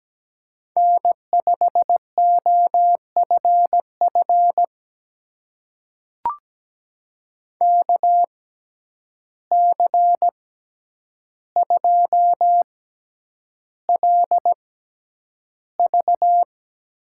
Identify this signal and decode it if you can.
Morse code, heard as a keyed tone, about 17 wpm, spaced as for 9 wpm, 700 Hz. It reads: N5OFF KC2LV
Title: USA Call Signs - 6x Character Spacing - 17wpm